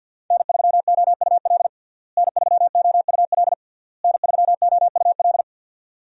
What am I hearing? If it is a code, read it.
Morse code, heard as a keyed tone, about 48 wpm, 700 Hz. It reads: N3OUL N3OUL N3OUL
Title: USA Call Signs - 5 Characters - Repeated 3x - 50wpm